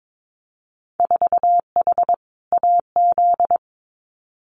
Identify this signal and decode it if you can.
Morse code, heard as a keyed tone, about 22 wpm, 700 Hz. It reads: 4H AZ
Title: Winter Field Day Exchanges Encore 22wpm